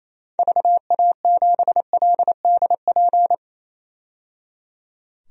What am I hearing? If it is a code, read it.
Morse code, heard as a keyed tone, about 28 wpm, 700 Hz. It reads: VA7LDP
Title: DXCC-100 Call Signs - 6 Characters - 28wpm